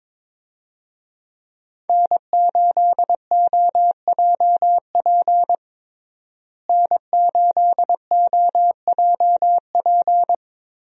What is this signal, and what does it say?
Morse code, heard as a keyed tone, about 22 wpm, 700 Hz. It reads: N8OJP N8OJP
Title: USA Call Signs - 5 Characters - Repeated 3x - 22wpm